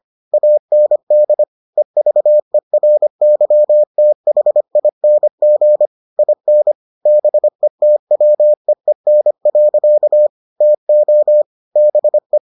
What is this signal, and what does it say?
Morse code, heard as a keyed tone, about 25 wpm, 600 Hz. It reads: AND EVERYTHING IN BETWEEN. TO BE